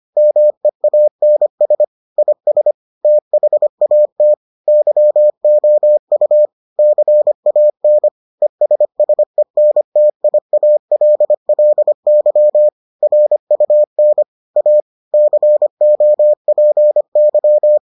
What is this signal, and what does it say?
Morse code, heard as a keyed tone, about 25 wpm, 600 Hz. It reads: MEANS IS THAT YOU CAN ESSENTIALLY RUN A COPY